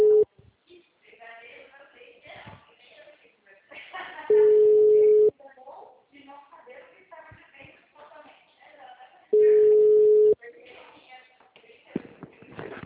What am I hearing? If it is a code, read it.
Ringback tone (Europe, 425 Hz), with speech.